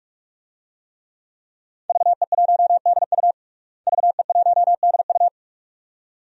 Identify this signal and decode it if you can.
Morse code, heard as a keyed tone, about 41 wpm, 700 Hz. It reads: VE1DU VE1DU
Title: DXCC-100 Call Signs - 5 Characters - Repeated 2x - 45wpm